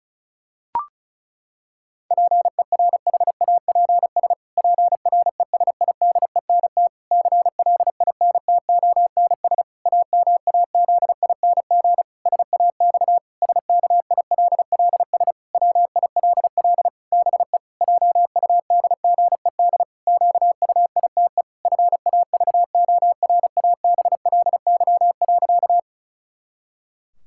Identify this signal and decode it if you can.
Morse code, heard as a keyed tone, about 34 wpm, 700 Hz. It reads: PERHAPS PRESIDENT CLINTONS AMAZING SAX SKILLS WILL BE JUDGED QUITE FAVORABLY.
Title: Pangram Challenge v2 - 35wpm